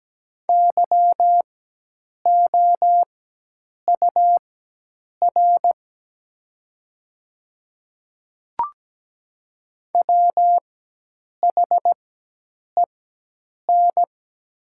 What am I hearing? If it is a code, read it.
Morse code, heard as a keyed tone, about 17 wpm, spaced as for 8 wpm, 700 Hz. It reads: YOUR WHEN
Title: 4x Character Spacing - Top 200 Words - 17wpm